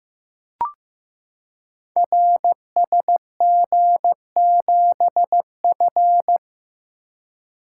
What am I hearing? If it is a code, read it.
Morse code, heard as a keyed tone, about 15 wpm, 700 Hz. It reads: RSG7F
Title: Code Groups - 5 Letter:Numbers - 15wpm